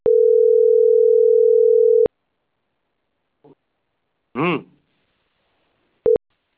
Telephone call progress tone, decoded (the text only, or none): ringback tone